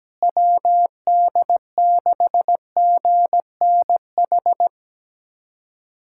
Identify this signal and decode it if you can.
Morse code, heard as a keyed tone, about 17 wpm, 700 Hz. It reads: WD6GNH
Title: USA Callsign DE Callsign 17wpm